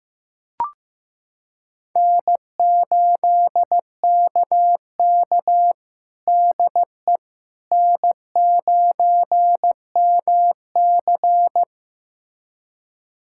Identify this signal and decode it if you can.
Morse code, heard as a keyed tone, about 15 wpm, 700 Hz. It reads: N8KK DE N9MC
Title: USA Callsign DE Callsign - 4 Characters - 15wpm